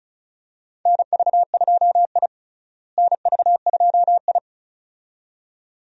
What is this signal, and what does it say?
Morse code, heard as a keyed tone, about 36 wpm, 700 Hz. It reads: NV2I NV2I
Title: USA Call Signs - 4 Characters - Repeated 2x - 35wpm